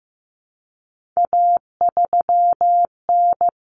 Morse code, heard as a keyed tone, about 15 wpm, 700 Hz. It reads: A3N
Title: Common Cut Numbers - 3-Digit 15wpm